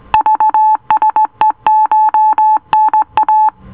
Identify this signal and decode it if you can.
Morse code, heard as a keyed tone, about 18 wpm, 870 Hz. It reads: VSECHNA